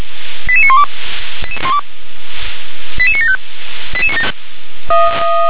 Five-tone selective call (ZVEI), heard as quick runs of five tones, tones of about 70 ms. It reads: 88912 88975